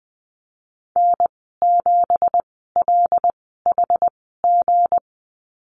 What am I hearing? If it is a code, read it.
Morse code, heard as a keyed tone, about 20 wpm, spaced as for 15 wpm, 700 Hz. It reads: N7LHG
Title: 167 Lesson 27g - Number 7 - Call Signs 20wpm